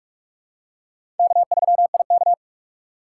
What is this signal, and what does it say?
Morse code, heard as a keyed tone, about 46 wpm, 700 Hz. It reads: K3IK
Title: USA Call Signs - 4 Characters - 6x Character Spacing - 45wpm